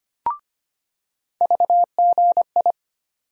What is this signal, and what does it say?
Morse code, heard as a keyed tone, about 25 wpm, 700 Hz. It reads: VGI
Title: USA Call Sign Postfixes - 3 Letters 25wpm